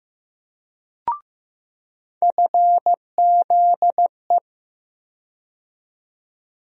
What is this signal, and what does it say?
Morse code, heard as a keyed tone, about 15 wpm, 700 Hz. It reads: FZE
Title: USA Call Sign Postfixes - 3 Letters 15wpm